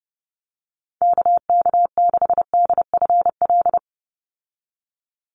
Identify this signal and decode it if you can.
Morse code, heard as a keyed tone, about 30 wpm, 700 Hz. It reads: KK6DFL
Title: USA Call Signs 30wpm